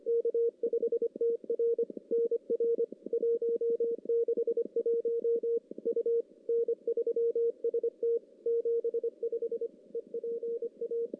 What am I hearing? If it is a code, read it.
Morse code, heard as a keyed tone, about 24 wpm, 460 Hz. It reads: K5TR NR 161 U N3ST 75 EPA
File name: contests / CW-SS-k5tr- -1600